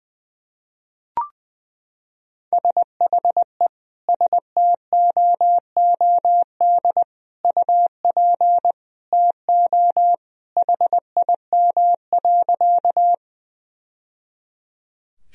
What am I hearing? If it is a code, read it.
Morse code, heard as a keyed tone, about 21 wpm, 700 Hz. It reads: SHE STOOD UP TO HIM.
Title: Sentences from Top 1000 Words - Review Part 2 - 20wpm